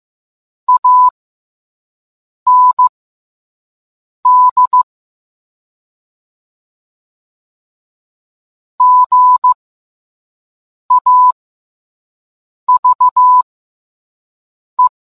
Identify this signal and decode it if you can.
Morse code, heard as a keyed tone, about 15 wpm, spaced as for 5 wpm, 1000 Hz. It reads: AND GAVE